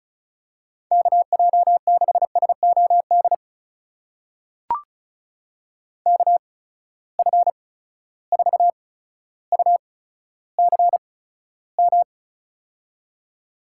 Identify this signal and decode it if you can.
Morse code, heard as a keyed tone, about 36 wpm, spaced as for 15 wpm, 700 Hz. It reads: KJ6SOD KF4UCM
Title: USA Call Signs - 6 Characters - 8x Character Spacing - 35wpm